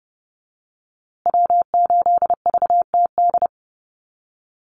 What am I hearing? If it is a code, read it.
Morse code, heard as a keyed tone, about 30 wpm, 700 Hz. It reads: W8VTD
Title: USA Call Signs Repeated 3x 30wpm